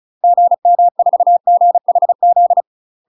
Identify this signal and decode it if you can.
Morse code, heard as a keyed tone, about 35 wpm, 700 Hz. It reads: GM4GHZ
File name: UK MASTER QSOs TRAINING / 35WPM / UK MASTER QSOs 0039